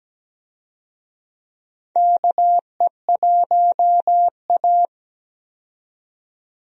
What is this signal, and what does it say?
Morse code, heard as a keyed tone, about 17 wpm, 700 Hz. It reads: KE1A
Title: USA Call Signs - 4 Characters - 17wpm